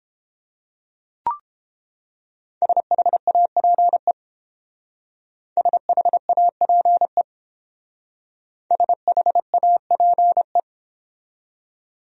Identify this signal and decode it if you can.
Morse code, heard as a keyed tone, about 30 wpm, 700 Hz. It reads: SHAPE SHAPE SHAPE